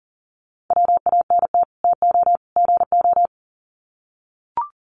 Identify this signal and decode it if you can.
Morse code, heard as a keyed tone, about 40 wpm, 700 Hz. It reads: WANT TO GO
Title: Sets of 3 Words 40wpm